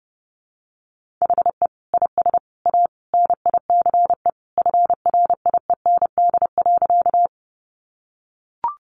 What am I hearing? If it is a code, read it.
Morse code, heard as a keyed tone, about 30 wpm, 700 Hz. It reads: HE IS A NICE FRIEND.
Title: Sentences from Top 1100 Words 30wpm